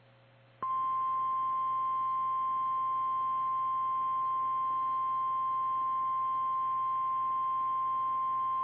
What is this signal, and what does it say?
Emergency Alert System (SAME) tones (the 1050 Hz attention tone). It is heard from 1.2 to 7.8 s.
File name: eas / KKPRKXPN / KKPRKXPN